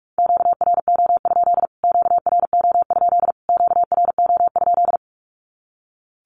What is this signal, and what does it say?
Morse code, heard as a keyed tone, about 44 wpm, 700 Hz. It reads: QRO? QRO? QRO?
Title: Q-Codes Repeated 3x 45wpm